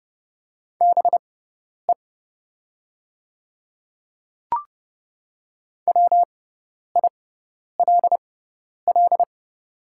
Morse code, heard as a keyed tone, about 30 wpm, spaced as for 10 wpm, 700 Hz. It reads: BE WILL
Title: 6x Character Spacing - Top 100 Words - 30wpm